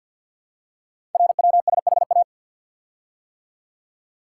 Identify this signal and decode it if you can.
Morse code, heard as a keyed tone, about 51 wpm, 700 Hz. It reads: RWSHA
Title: Code Groups - 5 Letters - 50wpm